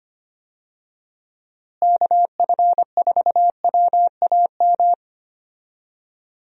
Morse code, heard as a keyed tone, about 25 wpm, 700 Hz. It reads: KF4WAM
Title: USA Call Signs - 6 Characters - Repeated 2x - 25wpm